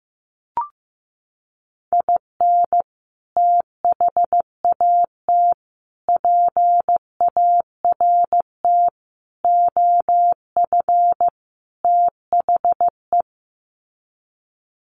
Morse code, heard as a keyed tone, about 15 wpm, 700 Hz. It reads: IN THAT PART OF THE
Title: Sets of 5 Words Encore2 15wpm